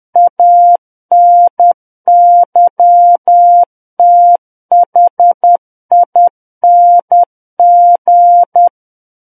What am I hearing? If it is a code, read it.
Morse code, heard as a keyed tone, about 10 wpm, 700 Hz. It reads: ANYTHING
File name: Book One / 01-Chapter 1 - 10 WPM